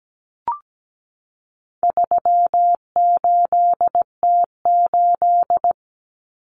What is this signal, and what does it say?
Morse code, heard as a keyed tone, about 17 wpm, 700 Hz. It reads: 38T8
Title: Common Cut Numbers - 4-Digit 17wpm